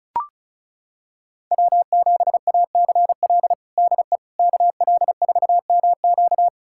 Morse code, heard as a keyed tone, about 34 wpm, 700 Hz. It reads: W7ACL DE KL4MQ
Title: USA Callsign DE Callsign - 5 Characters - 35wpm